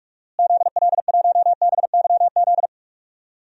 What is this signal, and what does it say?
Morse code, heard as a keyed tone, about 45 wpm, 700 Hz. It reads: ZL1BYZ
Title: DXCC-100 Call Signs - 6 Characters - Repeated 2x - 45wpm